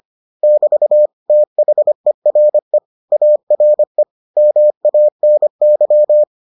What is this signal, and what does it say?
Morse code, heard as a keyed tone, about 25 wpm, 600 Hz. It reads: <BT> THERE ARE MANY